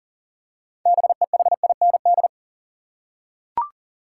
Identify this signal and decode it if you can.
Morse code, heard as a keyed tone, about 40 wpm, 700 Hz. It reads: BEHIND
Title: Six Letter Words - 40wpm